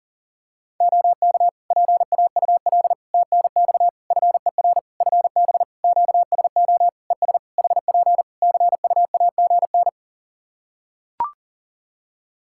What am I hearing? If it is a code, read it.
Morse code, heard as a keyed tone, about 40 wpm, 700 Hz. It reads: OK PAUL TNX FER FB QSO ES HP CUAGN
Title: K1USN SST QSO Practice 40wpm